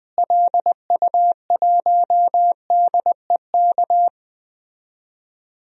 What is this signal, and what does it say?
Morse code, heard as a keyed tone, about 20 wpm, 700 Hz. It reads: LU1DEK
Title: DXCC-100 Call Signs - 6 Characters - Repeated 3x - 20wpm